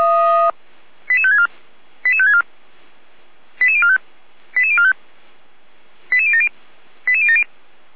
Five-tone selective call (ZVEI), heard as quick runs of five tones, tones of about 70 ms. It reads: 88564 88564 88045 88045 88070 88070
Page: /455/SDR/ZVEI/